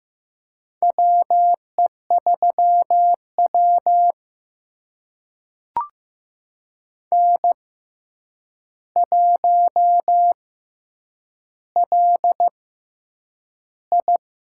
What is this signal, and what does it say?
Morse code, heard as a keyed tone, about 15 wpm, spaced as for 8 wpm, 700 Hz. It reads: WE3W N1LI
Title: USA Call Signs - 4 Characters - 6x Character Spacing - 15wpm